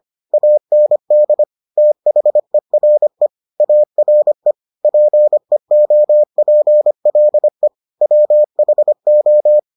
Morse code, heard as a keyed tone, about 25 wpm, 600 Hz. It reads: AND THERE ARE PEOPLE WHO